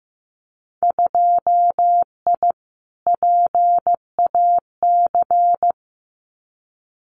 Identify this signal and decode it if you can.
Morse code, heard as a keyed tone, about 15 wpm, 700 Hz. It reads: 2I PAC
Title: Winter Field Day Exchanges Encore 15wpm